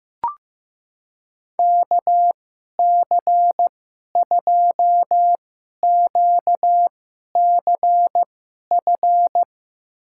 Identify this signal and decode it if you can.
Morse code, heard as a keyed tone, about 15 wpm, spaced as for 11 wpm, 700 Hz. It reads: KC2QCF
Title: USA Call Signs - 6 Characters - 2x Character Spacing - 15wpm